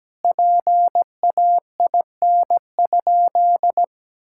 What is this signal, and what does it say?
Morse code, heard as a keyed tone, about 17 wpm, 700 Hz. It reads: PAIN?
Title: Sentences from Top 1600 Words 17wpm